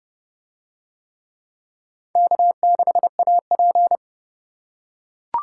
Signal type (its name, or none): Morse code